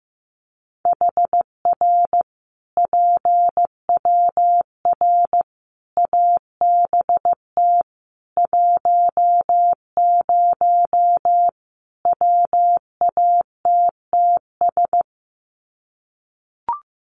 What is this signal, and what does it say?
Morse code, heard as a keyed tone, about 15 wpm, 700 Hz. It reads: HR PWR ABT 10 WATTS
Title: Common QSO Phrases 15wpm